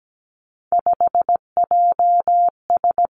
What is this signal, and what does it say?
Morse code, heard as a keyed tone, about 17 wpm, 700 Hz. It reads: 5JS
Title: Three Challenging Letter-Number Combinations 17wpm